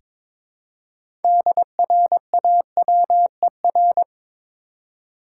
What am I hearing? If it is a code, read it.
Morse code, heard as a keyed tone, about 22 wpm, 700 Hz. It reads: DRAWER
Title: Six Letter Words - 22wpm